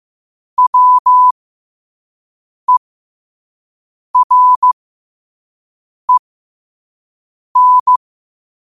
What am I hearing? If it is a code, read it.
Morse code, heard as a keyed tone, about 15 wpm, spaced as for 5 wpm, 1000 Hz. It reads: WEREN
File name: qotdfiles / 5wpm